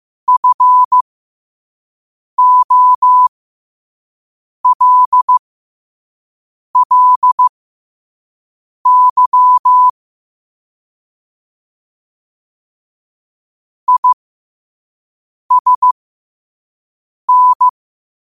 Morse code, heard as a keyed tone, about 15 wpm, spaced as for 5 wpm, 1000 Hz. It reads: FOLLY ISN
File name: qotdfiles / 5wpm